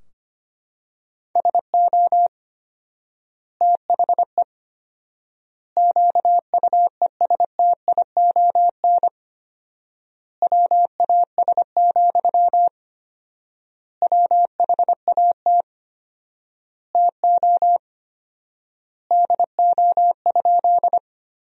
Morse code, heard as a keyed tone, about 25 wpm, 700 Hz. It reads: SO THE QUESTION WAS, WHAT TO DO?
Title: 4x Word-Spacing - Sentences from Top 400 Words - 25wpm